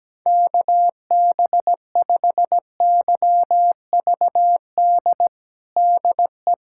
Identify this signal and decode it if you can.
Morse code, heard as a keyed tone, about 17 wpm, 700 Hz. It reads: KB5YVD DE
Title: USA Callsign DE Callsign - 6 Characters - 17wpm